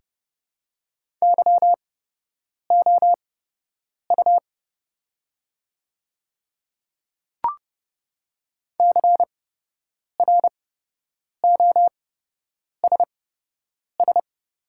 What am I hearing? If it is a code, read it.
Morse code, heard as a keyed tone, about 30 wpm, spaced as for 8 wpm, 700 Hz. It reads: YOU CROSS
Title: 8x Character Spacing - Top 300 Words - 30wpm